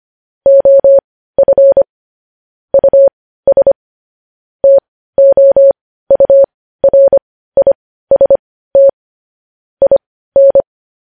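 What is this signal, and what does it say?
Morse code, heard as a keyed tone, about 25 wpm, spaced as for 15 wpm, 550 Hz. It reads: OF US TOURIST IN
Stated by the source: Morse code news headlines sent at 15 WPM.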